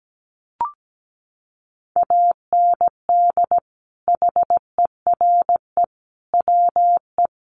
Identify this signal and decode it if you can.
Morse code, heard as a keyed tone, about 17 wpm, 700 Hz. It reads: AND HERE WE
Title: A Sound Of Thunder 17wpm